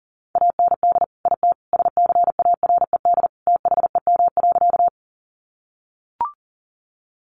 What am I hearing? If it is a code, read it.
Morse code, heard as a keyed tone, about 40 wpm, 700 Hz. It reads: AND IT SCARED THEM.
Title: Sentences from Top 700 Words 40wpm